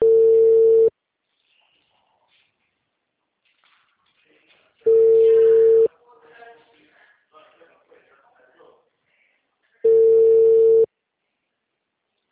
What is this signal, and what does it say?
Ringback tone (China, 450 Hz), with speech.